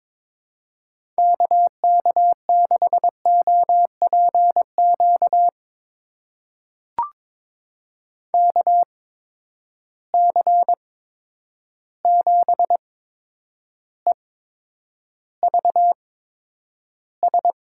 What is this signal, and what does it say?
Morse code, heard as a keyed tone, about 22 wpm, spaced as for 9 wpm, 700 Hz. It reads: KK6OPQ KC7EVS
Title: USA Call Signs - 6 Characters - 8x Character Spacing - 22wpm